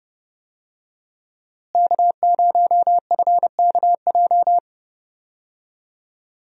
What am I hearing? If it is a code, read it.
Morse code, heard as a keyed tone, about 29 wpm, 700 Hz. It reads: K0FKJ
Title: USA Call Signs - 5 Characters - Repeated 2x - 30wpm